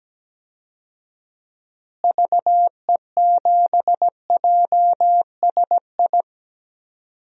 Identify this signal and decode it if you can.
Morse code, heard as a keyed tone, about 17 wpm, 700 Hz. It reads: VE7JSI
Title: DXCC-100 Call Signs - 6 Characters - Repeated 3x - 17wpm